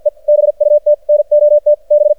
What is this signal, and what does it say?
Morse code, heard as a keyed tone, about 48 wpm, 600 Hz. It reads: E BUT NOT B
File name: archived / CRSnet